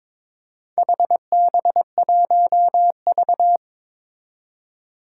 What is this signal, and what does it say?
Morse code, heard as a keyed tone, about 22 wpm, 700 Hz. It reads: HB1V
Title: Four Challenging Letter-Number Combinations 22wpm